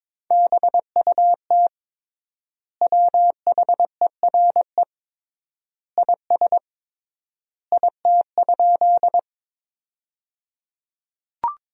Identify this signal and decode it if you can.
Morse code, heard as a keyed tone, about 22 wpm, 700 Hz. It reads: BUT WHERE IS IT?
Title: 3x Word-Spacing - Sentences from Top 200 Words - 22wpm